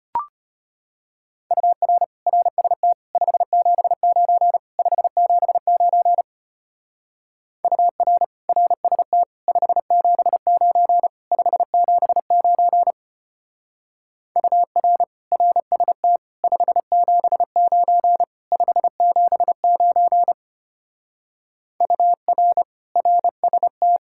Morse code, heard as a keyed tone, about 30 wpm, 700 Hz. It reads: UR RST 579 579 UR RST 579 579 UR RST 579 579 UR RST
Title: Speed-Racing Common QSO Phrases - 25wpm